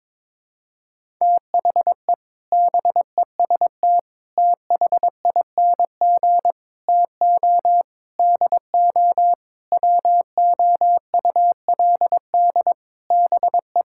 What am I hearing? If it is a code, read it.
Morse code, heard as a keyed tone, about 21 wpm, 700 Hz. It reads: THE BEST THING TO DO WOULD BE
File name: practice / 22wpm / Sentences with 8-Word Phrases - 22wpm